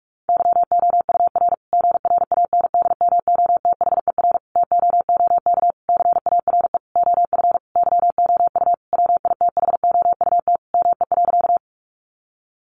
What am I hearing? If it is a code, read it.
Morse code, heard as a keyed tone, about 44 wpm, 700 Hz. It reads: YOUR GRANDMOTHER TOOK CARE OF YOU WITHOUT ME.
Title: Sentences from Top 1200 Words 45wpm